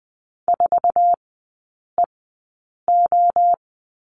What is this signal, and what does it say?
Morse code, heard as a keyed tone, about 20 wpm, spaced as for 8 wpm, 700 Hz. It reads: 4EO
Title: 107 Lesson 18e - Letter m - Three Characters 20wpm